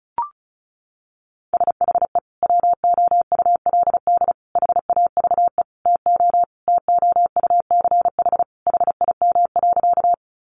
Morse code, heard as a keyed tone, about 34 wpm, 700 Hz. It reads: SHE WOULD HAVE TO TOUCH HIM.
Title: Sentences from Top 1200 Words 35wpm